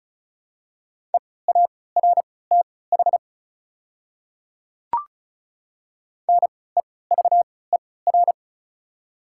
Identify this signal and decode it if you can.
Morse code, heard as a keyed tone, about 34 wpm, spaced as for 20 wpm, 700 Hz. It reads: EARTH NEVER
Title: 3x Character Spacing - Top 300 Words - 35wpm